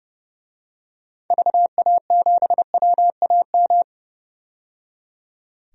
Morse code, heard as a keyed tone, about 30 wpm, 700 Hz. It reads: VA7WAM
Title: DXCC-100 Call Signs - 6 Characters - 30wpm